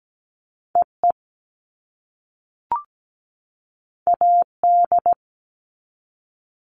Morse code, heard as a keyed tone, about 17 wpm, 700 Hz. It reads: EE AD
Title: All Cut Numbers - 2-Digit 17wpm